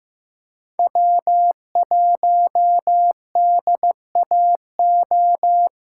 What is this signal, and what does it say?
Morse code, heard as a keyed tone, about 15 wpm, 700 Hz. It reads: W1DAO
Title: USA Callsign DE Callsign - 5 Characters - 15wpm